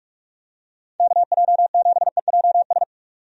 Morse code, heard as a keyed tone, about 44 wpm, 700 Hz. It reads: KJ7EJS
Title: USA Call Signs - 6 Characters - 45wpm